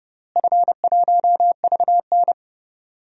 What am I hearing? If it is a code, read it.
Morse code, heard as a keyed tone, about 30 wpm, 700 Hz. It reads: F1VN
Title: DXCC-100 Call Signs - 4 Characters - Repeated 3x - 30wpm